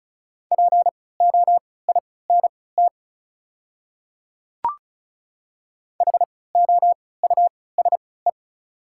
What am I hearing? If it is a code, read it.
Morse code, heard as a keyed tone, about 34 wpm, spaced as for 20 wpm, 700 Hz. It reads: POINT HOUSE
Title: 3x Character Spacing - Top 200 Words - 35wpm